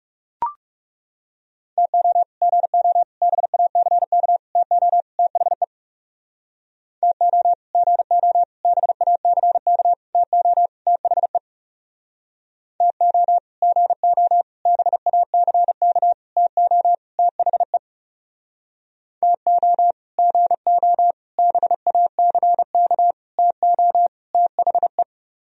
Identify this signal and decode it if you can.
Morse code, heard as a keyed tone, about 36 wpm, 700 Hz. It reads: TO GO BACK TO THE TO GO BACK TO THE TO GO BACK TO THE TO GO BACK TO THE
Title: Speed-Racing - Sets of 5 Words 30wpm